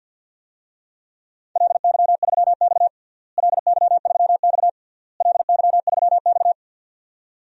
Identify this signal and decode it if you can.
Morse code, heard as a keyed tone, about 51 wpm, 700 Hz. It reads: LY3X LY3X LY3X
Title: DXCC-100 Call Signs - 4 Characters - Repeated 3x - 50wpm